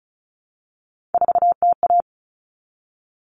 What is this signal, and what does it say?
Morse code, heard as a keyed tone, about 34 wpm, 700 Hz. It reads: 4TA
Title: All Cut Numbers - 3-Digit 35wpm